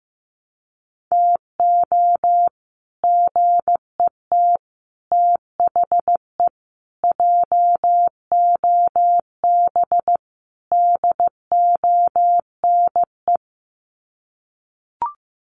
Morse code, heard as a keyed tone, about 15 wpm, 700 Hz. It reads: TO GET THE JOB DONE
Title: Sets of 5 Words Encore2 15wpm